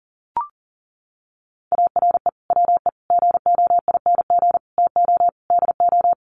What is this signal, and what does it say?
Morse code, heard as a keyed tone, about 40 wpm, 700 Hz. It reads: ARE WE GOING TO DO
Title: Sets of 5 Words 40wpm